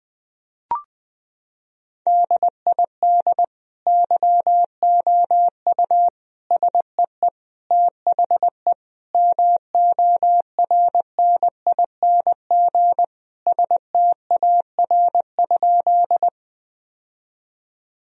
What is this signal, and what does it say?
Morse code, heard as a keyed tone, about 20 wpm, 700 Hz. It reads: DID YOU SEE THE MORNING STAR?
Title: Sentences from Top 1000 Words - Review Part 2 - 20wpm